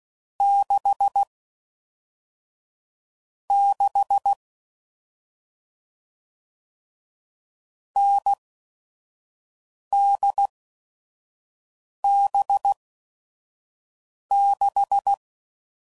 Morse code, heard as a keyed tone, about 16 wpm, spaced as for 5 wpm, 800 Hz. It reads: 6 6 NDB6
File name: Mixed / Short Runs / awj1ndb6gz7